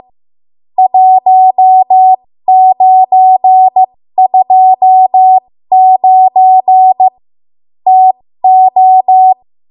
Morse code, heard as a keyed tone, about 15 wpm, spaced as for 13 wpm, 750 Hz. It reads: 1929 TO
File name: Morse / 13WPM